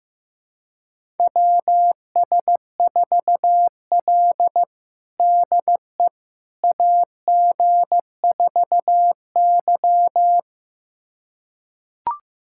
Morse code, heard as a keyed tone, about 15 wpm, 700 Hz. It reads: WS4L DE AG4Y
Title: USA Callsign DE Callsign - 4 Characters - 15wpm